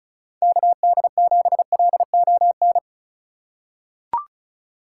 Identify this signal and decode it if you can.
Morse code, heard as a keyed tone, about 36 wpm, 700 Hz. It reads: KD7LON
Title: USA Call Signs - 6 Characters - 4x Character Spacing - 35wpm